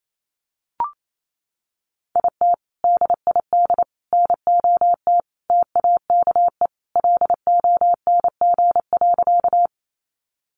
Morse code, heard as a keyed tone, about 28 wpm, 700 Hz. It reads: IT DID NOT TAKE LONG.
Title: Sentences from Top 600 Words 28wpm